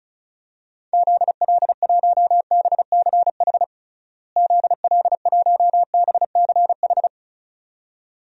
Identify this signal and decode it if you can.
Morse code, heard as a keyed tone, about 36 wpm, 700 Hz. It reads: ZL1BCH ZL1BCH
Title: DXCC-100 Call Signs - 6 Characters - Repeated 2x - 35wpm